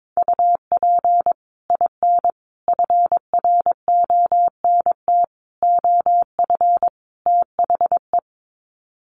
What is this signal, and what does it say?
Morse code, heard as a keyed tone, about 22 wpm, 700 Hz. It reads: UP IN FRONT OF THE
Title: Sets of 5 Words Encore2 22wpm